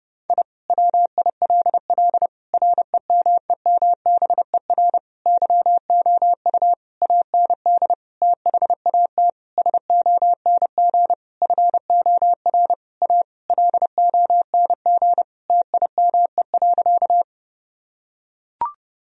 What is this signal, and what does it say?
Morse code, heard as a keyed tone, about 30 wpm, 700 Hz. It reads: I WILL REMEMBER YOU AND THAT SONG FOR A LONG TIME.
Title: Sentences from Top 1000 Words - Review Part 2 - 30wpm